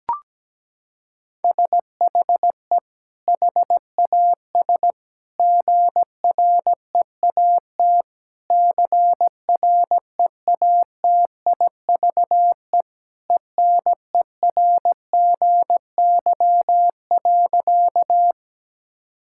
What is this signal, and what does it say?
Morse code, heard as a keyed tone, about 17 wpm, 700 Hz. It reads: SHE HAS GREAT CREATIVE ENERGY.
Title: Sentences from Top 1000 Words - Review Part 6 - 17wpm